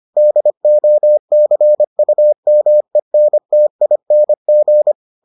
Morse code, heard as a keyed tone, about 25 wpm, 600 Hz. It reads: DOCUMENTING